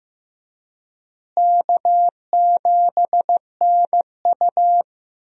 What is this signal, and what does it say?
Morse code, heard as a keyed tone, about 15 wpm, 700 Hz. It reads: K7NU
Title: USA Call Signs - 4 Characters - Repeated 3x - 15wpm